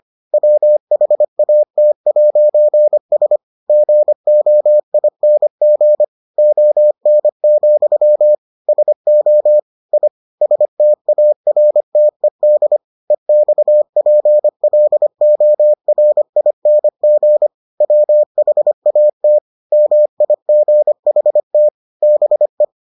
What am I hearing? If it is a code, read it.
Morse code, heard as a keyed tone, about 25 wpm, 600 Hz. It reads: WHAT'S GOING ON, SO I STARTED EXPLORING WHAT MIGHT BE